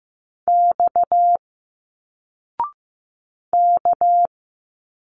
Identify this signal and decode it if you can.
Morse code, heard as a keyed tone, about 15 wpm, 700 Hz. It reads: X K
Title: Single Letters 15wpm